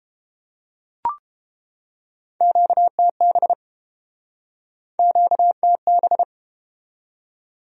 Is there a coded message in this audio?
Yes: Morse code from 2.4 to 6.2 s, heard as a keyed tone, about 31 wpm, 700 Hz. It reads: QTB QTB